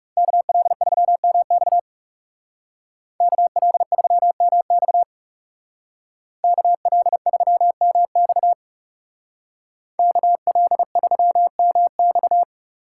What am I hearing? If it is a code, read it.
Morse code, heard as a keyed tone, about 34 wpm, 700 Hz. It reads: KL3MX KL3MX KL3MX KL3MX
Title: Speed Racing - USA 2x2 Call Signs 30wpm